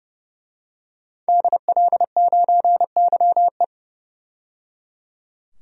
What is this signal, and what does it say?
Morse code, heard as a keyed tone, about 29 wpm, 700 Hz. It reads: DL9YE
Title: DXCC-100 Call Signs - 5 Characters - 30wpm